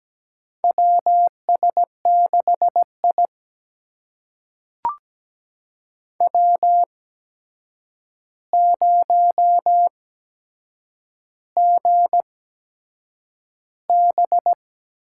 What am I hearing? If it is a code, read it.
Morse code, heard as a keyed tone, about 17 wpm, spaced as for 7 wpm, 700 Hz. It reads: WS6I W0GB
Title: USA Call Signs - 4 Characters - 8x Character Spacing - 17wpm